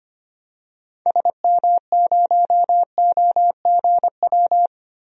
Morse code, heard as a keyed tone, about 25 wpm, 700 Hz. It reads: SM0OGW
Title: DXCC-100 Call Signs - 6 Characters - 25wpm